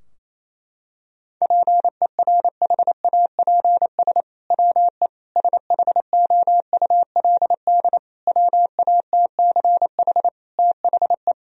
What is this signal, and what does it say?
Morse code, heard as a keyed tone, about 28 wpm, 700 Hz. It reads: PERHAPS WE SHOULD WATCH THE
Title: Sentences from Top 1000 Words - Review Part 4 - 28wpm